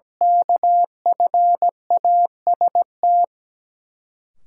Morse code, heard as a keyed tone, about 17 wpm, 700 Hz. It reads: KFAST
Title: Sentences from Top 1500 Words - New Word Drill Alt - 17wpm